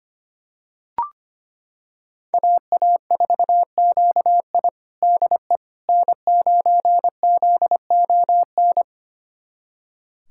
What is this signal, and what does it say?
Morse code, heard as a keyed tone, about 25 wpm, 700 Hz. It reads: AA4QI DE N9ZON
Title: USA Callsign DE Callsign - 5 Characters - 25wpm